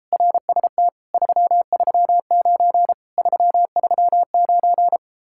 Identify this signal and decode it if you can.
Morse code, heard as a keyed tone, about 33 wpm, 700 Hz. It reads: RST 339 339